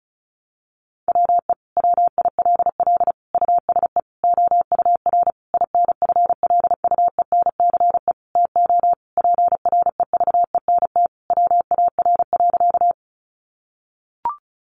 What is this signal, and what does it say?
Morse code, heard as a keyed tone, about 34 wpm, 700 Hz. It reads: WE WILL USE OUR INFLUENCE TO PREVENT WAR.
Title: Sentences from Top 1100 Words 35wpm